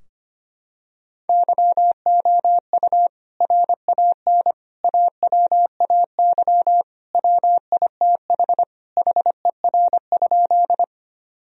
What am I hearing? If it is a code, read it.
Morse code, heard as a keyed tone, about 25 wpm, 700 Hz. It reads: YOU RAN AWAY WITH HER?
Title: Sentences from Top 1000 Words - Review Part 3 - 25wpm